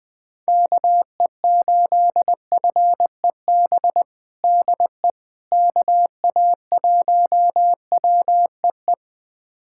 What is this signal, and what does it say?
Morse code, heard as a keyed tone, about 20 wpm, 700 Hz. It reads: KE8FEB DE KA1WEE
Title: USA Callsign DE Callsign - 6 Characters - 20wpm